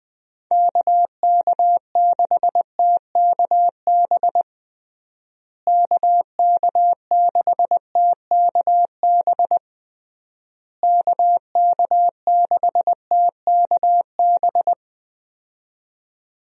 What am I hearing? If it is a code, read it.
Morse code, heard as a keyed tone, about 20 wpm, 700 Hz. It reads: KK6TKB KK6TKB KK6TKB
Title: USA Call Signs - 6 Characters - Repeated 3x - 20wpm